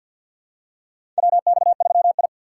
Morse code, heard as a keyed tone, about 50 wpm, 700 Hz. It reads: WX3I
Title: USA Call Signs - 4 Characters - 3x Character Spacing - 50wpm